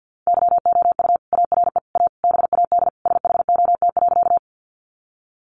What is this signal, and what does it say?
Morse code, heard as a keyed tone, about 49 wpm, 700 Hz. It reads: YOU ARE A BAD SHOT.
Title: Sentences from Top 1100 Words 50wpm